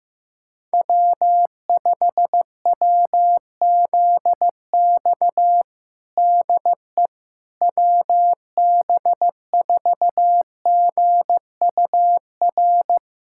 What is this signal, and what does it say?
Morse code, heard as a keyed tone, about 15 wpm, 700 Hz. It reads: W5WZX DE WB4GUR
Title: USA Callsign DE Callsign 15wpm